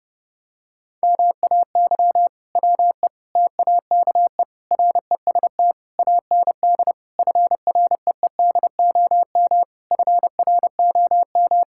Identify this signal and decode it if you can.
Morse code, heard as a keyed tone, about 30 wpm, 700 Hz. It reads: MAY WE TAKE REST AND FREEDOM FROM
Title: Sentences from Top 1600 Words 30wpm